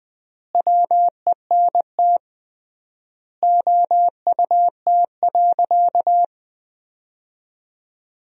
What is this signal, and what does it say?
Morse code, heard as a keyed tone, about 21 wpm, 700 Hz. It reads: WENT OUT.
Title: 3x Word-Spacing - Sentences from Top 300 Words - 20wpm